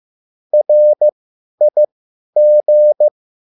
Morse code, heard as a keyed tone, about 15 wpm, spaced as for 10 wpm, 600 Hz. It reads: RIG
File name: QSL / CW lesson